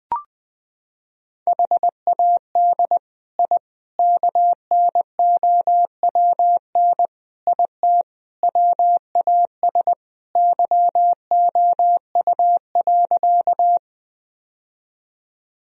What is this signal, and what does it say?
Morse code, heard as a keyed tone, about 21 wpm, 700 Hz. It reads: HAD I KNOWN IT WAS YOU.
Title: Sentences from Top 1000 Words - Review Part 3 - 20wpm